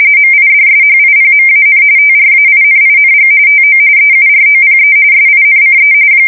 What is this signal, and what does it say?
RTTY, about 45.45 baud. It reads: MIDSHIPMAN TRAINING TO EUROPE SINCE
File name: fsk170 / battleshipnj